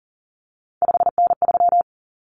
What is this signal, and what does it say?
Morse code, heard as a keyed tone, about 38 wpm, 700 Hz. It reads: 5N3
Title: Common Cut Numbers - 3-Digit 40wpm